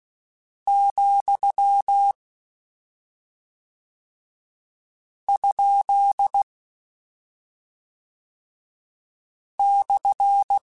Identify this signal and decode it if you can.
Morse code, heard as a keyed tone, about 16 wpm, 800 Hz. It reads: , ? /
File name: Misc / Misc-mp3s / Punct @10